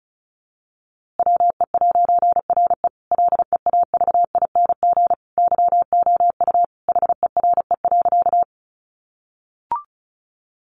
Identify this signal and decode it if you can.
Morse code, heard as a keyed tone, about 34 wpm, 700 Hz. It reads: WE'RE LEAVING YOU HERE.
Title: A Sound Of Thunder 35wpm